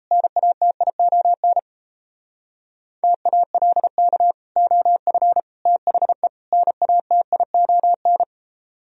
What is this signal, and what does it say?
Morse code, heard as a keyed tone, about 34 wpm, 700 Hz. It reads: NATION TALK OF THE NATION
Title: Speed-Racing - Sets of 4 Words 28wpm